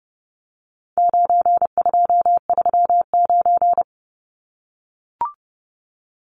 Morse code, heard as a keyed tone, about 30 wpm, 700 Hz. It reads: 9239
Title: Numbers - 4-Digit 30wpm